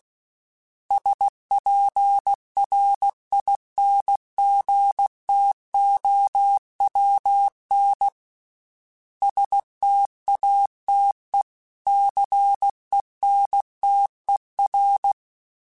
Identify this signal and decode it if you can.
Morse code, heard as a keyed tone, about 16 wpm, 800 Hz. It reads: SPRINGTOWN STATE CENTER
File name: fromSMs / FromSMs mp3s / Cities@15